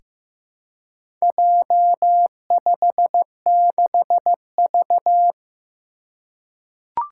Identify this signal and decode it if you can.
Morse code, heard as a keyed tone, about 15 wpm, 700 Hz. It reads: J56V
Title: Four Challenging Letter-Number Combinations 15wpm